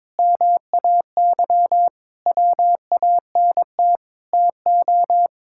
Morse code, heard as a keyed tone, about 21 wpm, 700 Hz. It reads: MAY WANT TO
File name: practice / 22wpm / Sentences with 8-Word Phrases - 22wpm